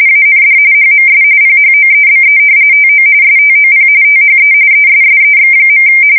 RTTY, about 45.45 baud. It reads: AND HMS PHOEBE TO STOP THE ESSEX.
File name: fsk170 / usnavy2